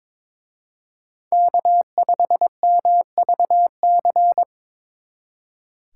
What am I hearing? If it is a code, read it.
Morse code, heard as a keyed tone, about 24 wpm, 690 Hz. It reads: K5MVC
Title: Speed Racing - USA Call Signs 22wpm